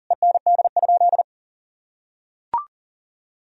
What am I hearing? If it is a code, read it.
Morse code, heard as a keyed tone, about 40 wpm, 700 Hz. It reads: END?
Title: Sentences from Top 1000 Words - Review Part 4 - 40wpm